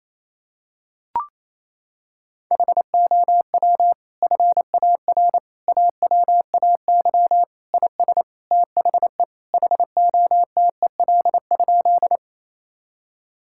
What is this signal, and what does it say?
Morse code, heard as a keyed tone, about 28 wpm, 700 Hz. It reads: HOW FAR AWAY IS THE HOTEL?
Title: Sentences from Top 1600 Words 28wpm